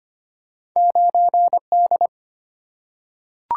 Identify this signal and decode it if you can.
Morse code, heard as a keyed tone, about 25 wpm, 700 Hz. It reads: 9D
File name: Two Letter-Number Combinations - 25wpm